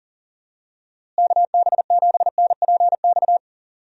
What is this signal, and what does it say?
Morse code, heard as a keyed tone, about 40 wpm, 700 Hz. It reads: KB7NPX
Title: USA Call Signs - 6 Characters - Repeated 3x - 40wpm